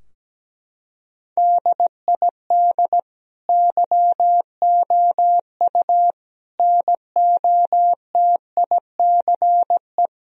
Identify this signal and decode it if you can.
Morse code, heard as a keyed tone, about 17 wpm, 700 Hz. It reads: DID YOU NOTICE
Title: Sentences from Top 1000 Words - Review Part 2 - 17wpm